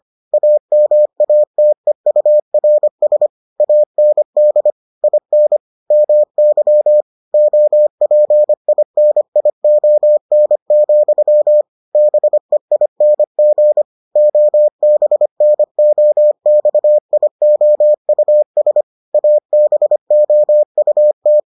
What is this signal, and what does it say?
Morse code, heard as a keyed tone, about 25 wpm, 600 Hz. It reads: AMATEURS AND IN MY OPINION, BEING OBNOXIOUS ABOUT